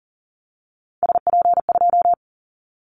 Morse code, heard as a keyed tone, about 40 wpm, 700 Hz. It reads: SP2
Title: Callsign Prefixes for Top 25 DXCC - 40wpm